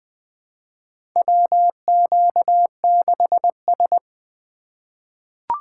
Morse code, heard as a keyed tone, about 20 wpm, 700 Hz. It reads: WQ6S
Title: USA Call Signs - 4 Characters - 4x Character Spacing - 20wpm